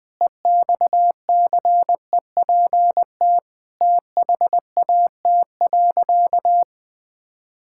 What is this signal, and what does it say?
Morse code, heard as a keyed tone, about 20 wpm, 700 Hz. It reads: EXCEPT THAT.
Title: Sentences from Top 1600 Words 20wpm